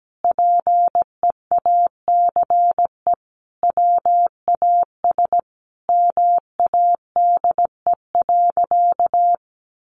Morse code, heard as a keyed tone, about 17 wpm, 700 Hz. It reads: PEACE WAS MADE.
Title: Sentences from Top 1500 Words 17wpm